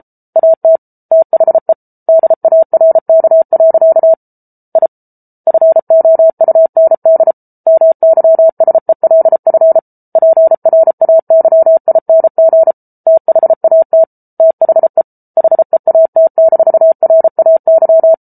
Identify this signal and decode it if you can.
Morse code, heard as a keyed tone, about 34 wpm, 650 Hz. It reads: AT THE DARK. I FOUND MYSELF PRAYING THAT THE HEAT-RAY